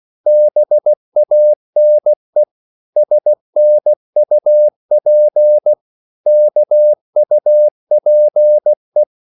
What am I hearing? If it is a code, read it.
Morse code, heard as a keyed tone, about 16 wpm, 600 Hz. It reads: BANE SNUP KUPE
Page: Korte danske ord 16wpm